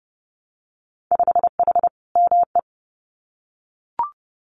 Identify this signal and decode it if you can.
Morse code, heard as a keyed tone, about 30 wpm, 700 Hz. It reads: 5H ME
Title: Winter Field Day Exchanges Encore 30wpm